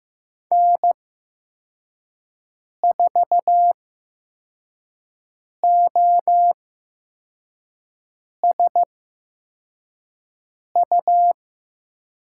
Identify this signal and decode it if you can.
Morse code, heard as a keyed tone, about 15 wpm, spaced as for 4 wpm, 700 Hz. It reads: N4OSU
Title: USA Call Signs - 5 Characters - 8x Character Spacing - 15wpm